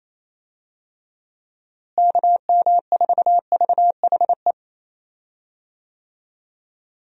Morse code, heard as a keyed tone, about 27 wpm, 700 Hz. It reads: KM4VHE
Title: USA Call Signs - 6 Characters - Repeated 2x - 28wpm